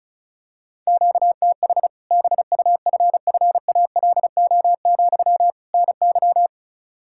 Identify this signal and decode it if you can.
Morse code, heard as a keyed tone, about 34 wpm, 700 Hz. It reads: QTH BUFFALO, NY
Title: Practice QSO - Exchange 1 - Beginning - 35wpm